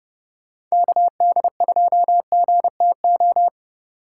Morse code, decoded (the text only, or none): KD2GTO